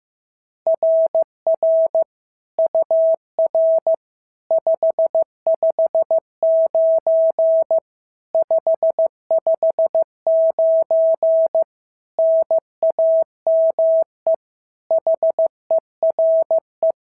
Morse code, heard as a keyed tone, about 14 wpm, 650 Hz. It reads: RR UR 559 559 NAME HERE
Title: SOTA QSO Practice 15wpm